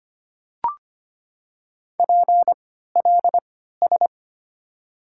Morse code, heard as a keyed tone, about 25 wpm, 700 Hz. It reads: P L S
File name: Top 90 QSO Elements - 3x Character Spacing - 25wpm